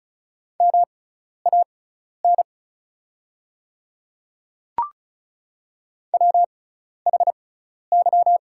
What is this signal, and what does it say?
Morse code, heard as a keyed tone, about 34 wpm, spaced as for 12 wpm, 690 Hz. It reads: MAN WHY
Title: 6x Character Spacing - Top 200 Words - 35wpm